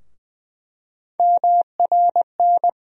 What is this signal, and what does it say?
Morse code, heard as a keyed tone, about 20 wpm, 700 Hz. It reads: MRN
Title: USA Call Sign Postfixes - 3 Letters 20wpm